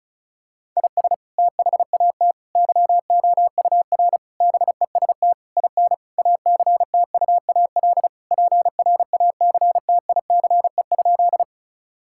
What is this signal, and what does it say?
Morse code, heard as a keyed tone, about 36 wpm, 700 Hz. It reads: IS THAT YOUR BEST IN ACTUAL PRACTICE?
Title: Sentences from Top 1000 Words - Review Part 5 - 35wpm